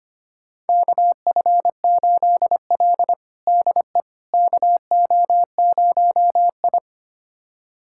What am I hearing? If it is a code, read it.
Morse code, heard as a keyed tone, about 25 wpm, 700 Hz. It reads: KF8L DE KO0I
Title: USA Callsign DE Callsign - 4 Characters - 25wpm